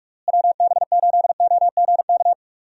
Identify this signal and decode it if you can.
Morse code, heard as a keyed tone, about 46 wpm, 700 Hz. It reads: WB8OGK
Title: USA Call Signs - 6 Characters - Repeated 3x - 45wpm